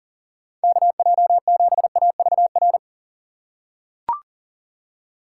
Morse code, heard as a keyed tone, about 40 wpm, 700 Hz. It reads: KJ7AVR
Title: USA Call Signs - 6x Character Spacing - 40wpm